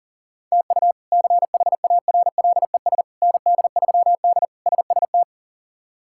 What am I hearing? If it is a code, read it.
Morse code, heard as a keyed tone, about 37 wpm, 700 Hz. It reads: TU CHARLES ND3D SST
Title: K1USN SST QSO Practice - 40wpm